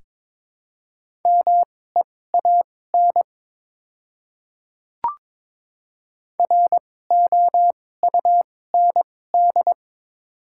Morse code, heard as a keyed tone, about 22 wpm, spaced as for 16 wpm, 700 Hz. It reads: MEAN ROUND
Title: 2x Character Spacing - Top 200 Words - 22wpm